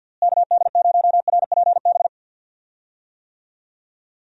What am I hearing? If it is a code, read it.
Morse code, heard as a keyed tone, about 50 wpm, 700 Hz. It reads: KD0RPB
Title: USA Call Signs - 6 Characters - Repeated 3x - 50wpm